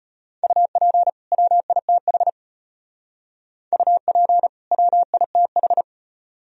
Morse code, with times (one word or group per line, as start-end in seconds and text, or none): 0.4-1.1 UP
1.3-2.3 WITH
3.7-4.5 UP
4.7-5.8 WITH